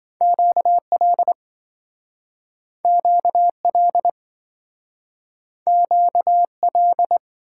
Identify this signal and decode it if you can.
Morse code, heard as a keyed tone, about 27 wpm, 700 Hz. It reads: QL QL QL